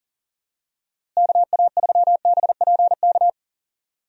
Morse code, heard as a keyed tone, about 40 wpm, 700 Hz. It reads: KA3BPK
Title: USA Call Signs - 6 Characters - Repeated 3x - 40wpm